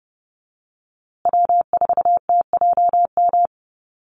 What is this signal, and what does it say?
Morse code, heard as a keyed tone, about 30 wpm, 700 Hz. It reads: W4TJM
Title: USA Call Signs Repeated 2x 30wpm